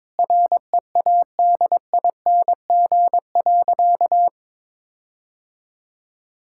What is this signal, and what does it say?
Morse code, heard as a keyed tone, about 22 wpm, 700 Hz. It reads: READING.
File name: Sentences with 7-Word Phrases - 22wpm